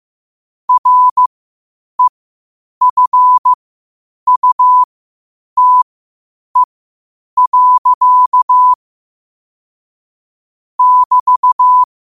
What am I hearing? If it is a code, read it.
Morse code, heard as a keyed tone, about 15 wpm, spaced as for 8 wpm, 1000 Hz. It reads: REFUTE. <BT>
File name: qotdfiles / 8wpm